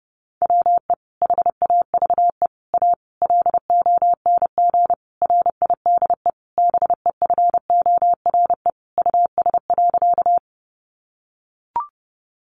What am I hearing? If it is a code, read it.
Morse code, heard as a keyed tone, about 28 wpm, 700 Hz. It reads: WE HAVE A LONG RIDE BEFORE US.
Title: Sentences from Top 1200 Words 30wpm